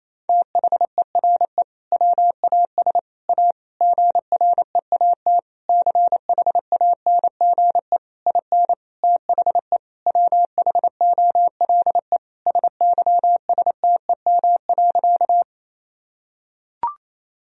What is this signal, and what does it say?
Morse code, heard as a keyed tone, about 27 wpm, 700 Hz. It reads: THERE WAS A GREAT CHANGE IN THE WHOLE SYSTEM.
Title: Sentences from Top 1000 Words - Review Part 3 - 28wpm